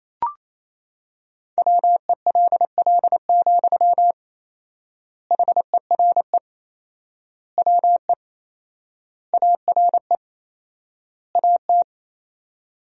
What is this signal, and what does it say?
Morse code, heard as a keyed tone, about 28 wpm, 700 Hz. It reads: WELL, HERE WE ARE AT
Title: 4x Word-Spacing - Sentences from Top 300 Words - 28wpm